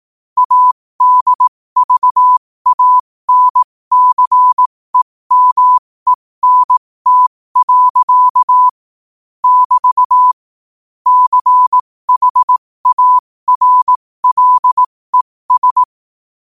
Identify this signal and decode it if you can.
Morse code, heard as a keyed tone, about 18 wpm, spaced as for 15 wpm, 1000 Hz. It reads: ADVANCEMENT. <BT> CHARLES
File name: qotdfiles / 15wpm